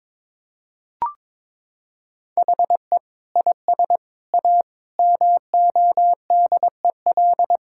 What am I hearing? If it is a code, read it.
Morse code, heard as a keyed tone, about 22 wpm, 700 Hz. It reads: HE IS A MODEL
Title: Sentences from Top 1600 Words 22wpm